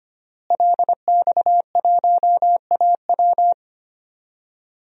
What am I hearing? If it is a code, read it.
Morse code, heard as a keyed tone, about 25 wpm, 700 Hz. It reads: LX1AW
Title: DXCC-100 Call Signs - 5 Characters - Repeated 3x - 25wpm